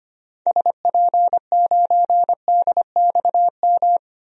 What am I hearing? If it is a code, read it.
Morse code, heard as a keyed tone, about 25 wpm, 700 Hz. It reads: SP9DXM
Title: DXCC-100 Call Signs - 6 Characters - 25wpm